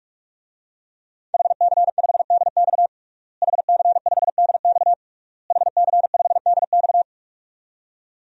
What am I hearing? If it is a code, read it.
Morse code, heard as a keyed tone, about 43 wpm, 700 Hz. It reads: HC5DX HC5DX HC5DX
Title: DXCC-100 Call Signs - 5 Characters - Repeated 3x - 45wpm